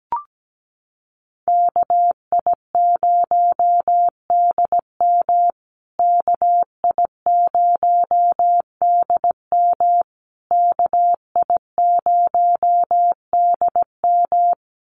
Morse code, heard as a keyed tone, about 17 wpm, 700 Hz. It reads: KI0DM KI0DM KI0DM
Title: USA Call Signs Repeated 3x 17wpm